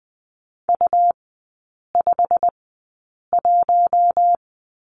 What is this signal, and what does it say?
Morse code, heard as a keyed tone, about 19 wpm, spaced as for 8 wpm, 700 Hz. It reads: U51
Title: 100 Lesson 17e - Letter u - Three Characters 20wpm